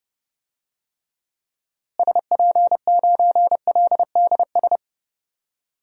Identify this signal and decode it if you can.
Morse code, heard as a keyed tone, about 30 wpm, 700 Hz. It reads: SP9LDS
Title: DXCC-100 Call Signs - 6 Characters - Repeated 2x - 30wpm